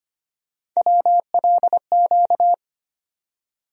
Morse code, heard as a keyed tone, about 25 wpm, 700 Hz. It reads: WLQ
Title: USA Call Sign Postfixes - 3 Letters 25wpm